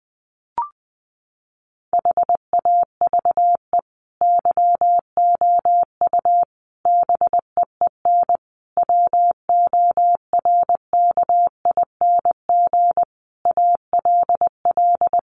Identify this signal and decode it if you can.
Morse code, heard as a keyed tone, about 20 wpm, 700 Hz. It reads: HAVE YOU BEEN WORKING ALL
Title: Sentences from Top 700 Words 20wpm